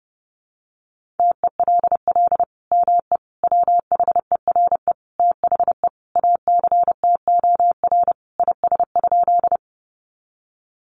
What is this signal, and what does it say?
Morse code, heard as a keyed tone, about 28 wpm, 700 Hz. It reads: TELL ME WHERE THE ACTOR IS?
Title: Sentences from Top 1500 Words 30wpm